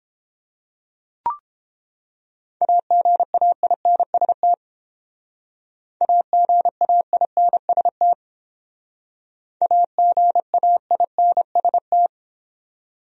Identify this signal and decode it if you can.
Morse code, heard as a keyed tone, about 30 wpm, 700 Hz. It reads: AGAINST AGAINST AGAINST